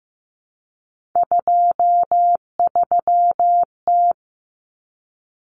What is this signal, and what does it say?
Morse code, heard as a keyed tone, about 15 wpm, 700 Hz. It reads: 23T
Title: Common Cut Numbers - 3-Digit 15wpm